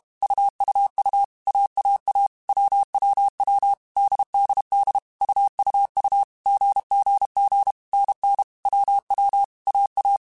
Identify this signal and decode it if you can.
Morse code, heard as a keyed tone, about 32 wpm, 800 Hz. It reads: UUU AAA WWW DDD UUU GGG NN WW AA
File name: Basic Sets / Basic Letter Groups Collective7 / AWUNDG@30